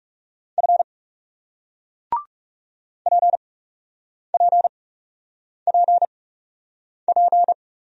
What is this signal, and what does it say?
Morse code, heard as a keyed tone, about 41 wpm, spaced as for 8 wpm, 700 Hz. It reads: F PPPP